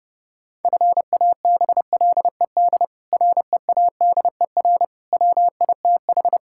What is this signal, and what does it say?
Morse code, heard as a keyed tone, about 30 wpm, 700 Hz. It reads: FABLED READER WITH
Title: Pangram Challenge v2 - 30wpm